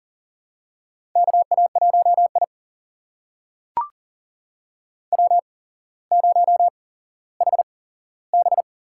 Morse code, heard as a keyed tone, about 40 wpm, spaced as for 17 wpm, 700 Hz. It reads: KA1I W0HB
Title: USA Call Signs - 8x Character Spacing - 40wpm